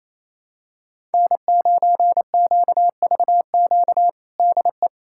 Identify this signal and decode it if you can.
Morse code, heard as a keyed tone, about 28 wpm, 700 Hz. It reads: N9QVQ DE
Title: USA Callsign DE Callsign - 5 Characters - 28wpm